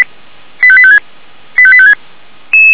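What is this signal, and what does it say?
Five-tone selective call (ZVEI), heard as quick runs of five tones, tones of about 70 ms. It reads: 85856 85856
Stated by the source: ZVEI five-tone selective call received on SDR, from the Feuerwehr Orlamünde (fire brigade) audio server.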